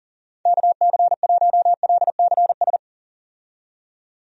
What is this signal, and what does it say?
Morse code, heard as a keyed tone, about 39 wpm, 700 Hz. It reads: KC1LCS
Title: USA Call Signs - 6 Characters - 40wpm